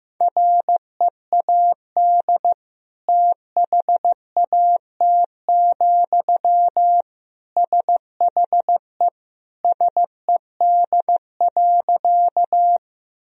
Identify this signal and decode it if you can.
Morse code, heard as a keyed tone, about 15 wpm, 700 Hz. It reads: READ THAT, SHE SED.
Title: Sentences from Top 200 Words and Abbreviations 15wpm